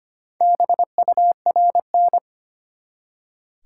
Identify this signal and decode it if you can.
Morse code, heard as a keyed tone, about 25 wpm, 700 Hz. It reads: BURN
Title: Four Letter Words - 25wpm